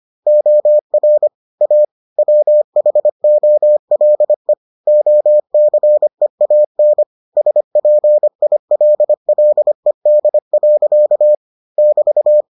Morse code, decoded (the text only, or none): OR A WHOLE OCEAN SPILLED. <BT>